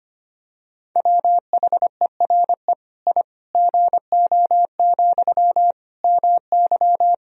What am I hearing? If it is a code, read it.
Morse code, heard as a keyed tone, about 25 wpm, 700 Hz. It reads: WHERE I GO, MY
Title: Sentences from Top 1000 Words - Review Part 1 - 25wpm